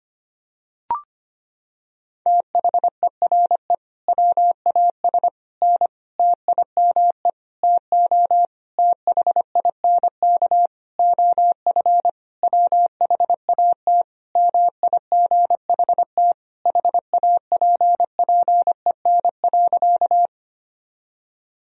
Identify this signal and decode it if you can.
Morse code, heard as a keyed tone, about 25 wpm, 700 Hz. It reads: THERE WAS N TIME TO THINK OF WHAT MIGHT HAPPEN.
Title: Sentences from Top 400 Words and Abbreviations 25wpm